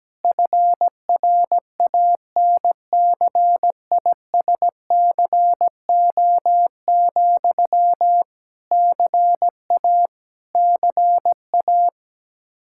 Morse code, heard as a keyed tone, about 17 wpm, 700 Hz. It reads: FRANCISCO, CA CA
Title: Practice QSO - Exchange 1 - Beginning - 17wpm